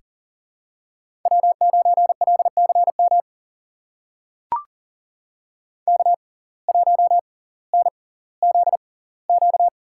Morse code, heard as a keyed tone, about 40 wpm, spaced as for 20 wpm, 700 Hz. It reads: W9LCM K1NZQ
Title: USA Call Signs - 5 Characters - 6x Character Spacing - 40wpm